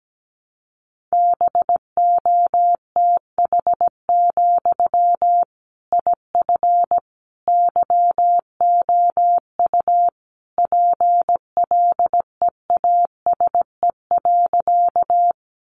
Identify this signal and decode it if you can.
Morse code, heard as a keyed tone, about 17 wpm, 700 Hz. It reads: BOTH, IF YOU PLEASE.
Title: Sentences from Top 1500 Words 17wpm